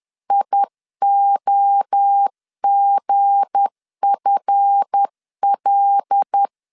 Morse code, heard as a keyed tone, about 10 wpm, 800 Hz. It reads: IOGFL
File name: LEZIONI / 7 GF9Z / 7 GF 10wpm 2514